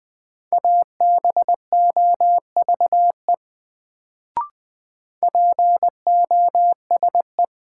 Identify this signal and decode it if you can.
Morse code, heard as a keyed tone, about 21 wpm, 700 Hz. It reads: ABOVE POSE
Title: Sentences from Top 1000 Words - Review Part 2 - New Word Drill - 20wpm